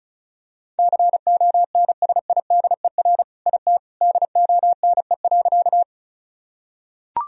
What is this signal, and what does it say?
Morse code, heard as a keyed tone, about 34 wpm, 700 Hz. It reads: CONSIDER IT DONE.
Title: Sentences from Top 1000 Words - Review Part 4 - 35wpm